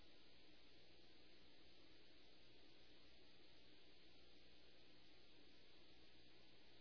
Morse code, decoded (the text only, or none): none